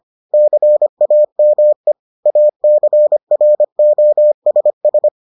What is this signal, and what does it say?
Morse code, heard as a keyed tone, about 25 wpm, 600 Hz. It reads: CAME ACROSS